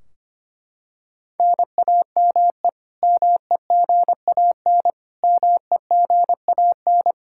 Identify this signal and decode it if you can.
Morse code, heard as a keyed tone, about 25 wpm, 700 Hz. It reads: NAME MEGAN MEGAN
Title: Practice QSO - Exchange 1 - Beginning - 25wpm